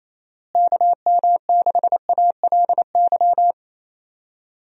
Morse code, heard as a keyed tone, about 28 wpm, 700 Hz. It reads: KM6ALY
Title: USA Call Signs - 6 Characters - Repeated 2x - 28wpm